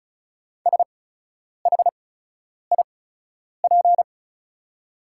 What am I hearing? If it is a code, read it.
Morse code, heard as a keyed tone, about 36 wpm, spaced as for 10 wpm, 700 Hz. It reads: SHIP
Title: 8x Character Spacing - Top 500 Words - 35wpm